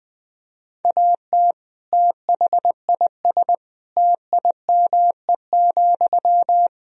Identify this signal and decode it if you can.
Morse code, heard as a keyed tone, about 20 wpm, 700 Hz. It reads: AT THIS TIME,
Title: Sentences from Top 1000 Words - Review Part 2 - 20wpm